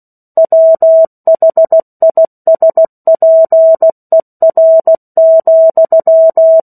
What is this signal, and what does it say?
Morse code, heard as a keyed tone, about 16 wpm, 650 Hz. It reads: WHISPER,
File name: 03-Chapter-03 16 WPM